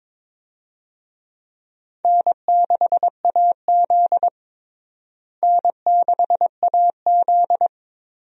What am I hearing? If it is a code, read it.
Morse code, heard as a keyed tone, about 22 wpm, 700 Hz. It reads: N6AZ N6AZ
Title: USA Call Signs - 4 Characters - Repeated 2x - 22wpm